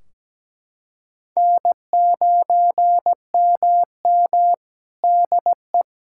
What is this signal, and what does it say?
Morse code, heard as a keyed tone, about 17 wpm, 700 Hz. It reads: N9MM DE
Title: Practice QSO - Exchange 4 - Ending - 17wpm